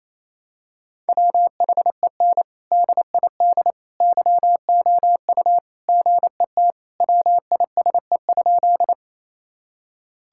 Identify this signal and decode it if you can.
Morse code, heard as a keyed tone, about 28 wpm, 700 Hz. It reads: WHEN DID YOU GET WISE?
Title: Sentences from Top 1600 Words 28wpm